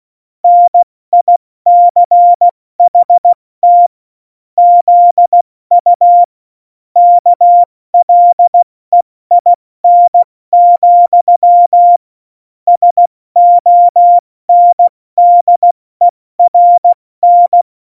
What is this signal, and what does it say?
Morse code, heard as a keyed tone, about 16 wpm, spaced as for 14 wpm, 700 Hz. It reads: NICHT ZU KLEIN, SONDERN
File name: Morsekurs / Witze / Witz 23. April 2019, 70 F80